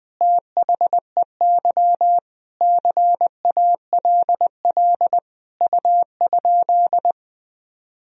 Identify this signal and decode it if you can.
Morse code, heard as a keyed tone, about 21 wpm, 700 Hz. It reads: THEY CALL U?
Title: Sentences from Top 100 Words and Abbreviations 20wpm